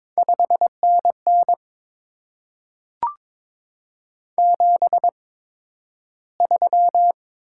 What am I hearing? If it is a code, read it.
Morse code, heard as a keyed tone, about 22 wpm, spaced as for 12 wpm, 700 Hz. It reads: 5NN 73
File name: Top 20 QSO Elements - 8x Character Spacing - 22wpm